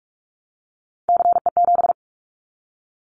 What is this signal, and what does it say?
Morse code, heard as a keyed tone, about 44 wpm, 700 Hz. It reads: CE7
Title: Callsign Prefixes for Top 100 DXCC - 45wpm